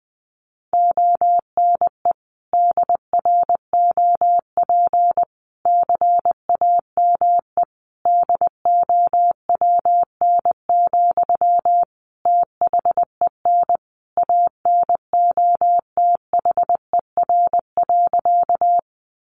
Morse code, heard as a keyed tone, about 20 wpm, 700 Hz. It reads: ONE DROP CAME DOWN, THEN ANOTHER.
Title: Sentences from Top 1400 Words 20wpm